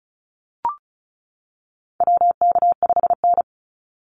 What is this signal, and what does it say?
Morse code, heard as a keyed tone, about 36 wpm, 700 Hz. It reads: WK5N
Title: Call Signs - 35wpm